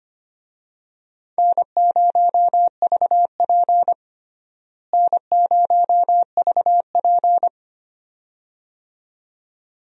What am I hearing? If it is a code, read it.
Morse code, heard as a keyed tone, about 25 wpm, 700 Hz. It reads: N0VP N0VP
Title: USA Call Signs - 4 Characters - Repeated 2x - 25wpm